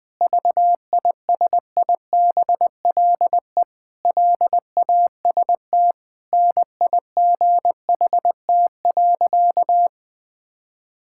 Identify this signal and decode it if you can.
Morse code, heard as a keyed tone, about 20 wpm, 700 Hz. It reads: VISIBLE LAST NIGHT.
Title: Sentences with 5-Word Phrases - Spoken Beginning - 20wpm